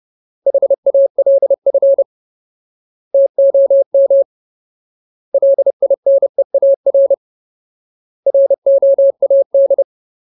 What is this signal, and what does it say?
Morse code, heard as a keyed tone, about 30 wpm, 550 Hz. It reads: HALF TOM LINEAR ROAD
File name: Ultimate Word List / 30WPM Ultimate